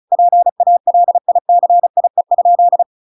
Morse code, heard as a keyed tone, about 34 wpm, 700 Hz. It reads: PALICIE?